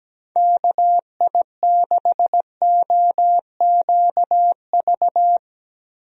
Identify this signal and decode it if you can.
Morse code, heard as a keyed tone, about 17 wpm, 700 Hz. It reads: KI6OQV
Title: USA Call Signs - 6 Characters - Repeated 2x - 17wpm